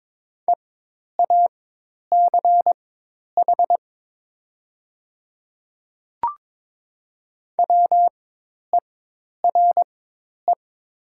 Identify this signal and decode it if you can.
Morse code, heard as a keyed tone, about 22 wpm, spaced as for 10 wpm, 700 Hz. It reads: EACH WERE
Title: 4x Character Spacing - Top 100 Words - 22wpm